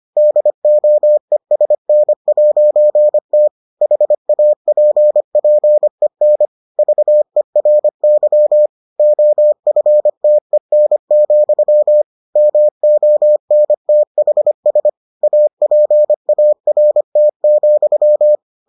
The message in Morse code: DOESN'T HAPPEN VERY OFTEN, MONTHS APART,

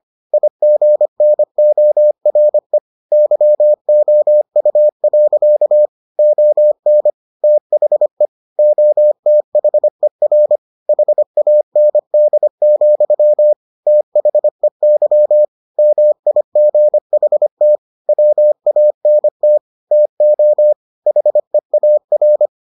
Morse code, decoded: IGNORE YOU. ON THE OTHER HAND, THEY MIGHT WANT TO HEAR

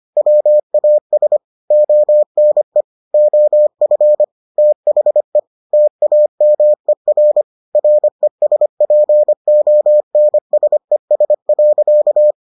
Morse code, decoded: WAS ONE OF THE TAMER RESPONSES.

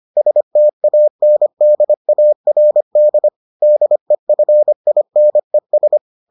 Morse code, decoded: STANDARD DEFINES

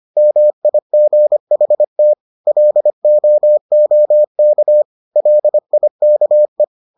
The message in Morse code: MIGHT LOOK LIKE

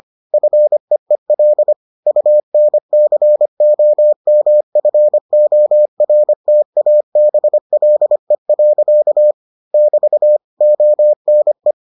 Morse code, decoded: FEEL UNCOMFORTABLE. <BT> ONE

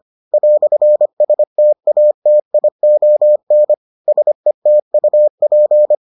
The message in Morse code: "STATION SETUP

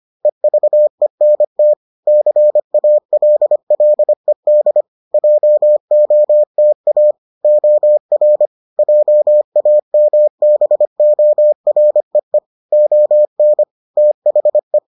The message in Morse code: EVENT CALLED JOTA OR JAMBOREE ON THE

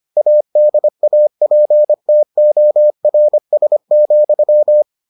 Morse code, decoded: ADAPTORS,

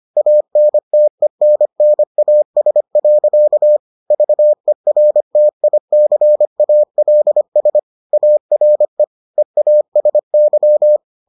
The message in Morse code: ANTENNAS. VERTICALS ARE EASY

